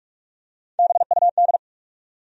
Morse code, heard as a keyed tone, about 43 wpm, 700 Hz. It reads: BUD
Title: USA Call Sign Postfixes - 3 Letters 45wpm